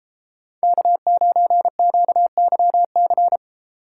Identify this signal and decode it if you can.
Morse code, heard as a keyed tone, about 30 wpm, 700 Hz. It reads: K9QYC